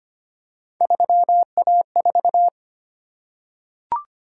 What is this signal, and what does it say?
Morse code, heard as a keyed tone, about 25 wpm, 700 Hz. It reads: 3A4
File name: Three Letter-Number Combinations - 25wpm